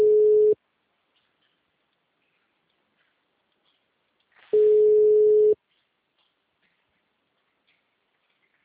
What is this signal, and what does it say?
Ringback tone (Europe, 425 Hz).
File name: out- -2002- - - .3